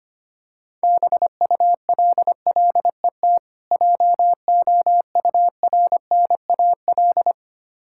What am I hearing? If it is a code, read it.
Morse code, heard as a keyed tone, about 25 wpm, 700 Hz. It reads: BULLET JOURNAL
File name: practice / 25wpm / Difficult Words with Hints - 25wpm